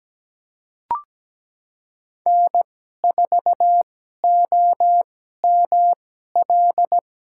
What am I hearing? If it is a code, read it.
Morse code, heard as a keyed tone, about 17 wpm, spaced as for 12 wpm, 700 Hz. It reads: N4OML
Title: USA Call Signs - 5 Characters - 2x Character Spacing - 17wpm